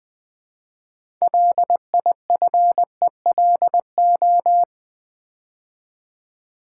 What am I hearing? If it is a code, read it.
Morse code, heard as a keyed tone, about 20 wpm, 700 Hz. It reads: LIFELO
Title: Build A Word - 2 Letters At A Time - 8-Letter Words - 20wpm